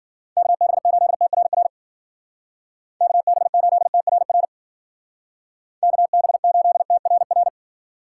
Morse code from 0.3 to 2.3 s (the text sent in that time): KB8TRR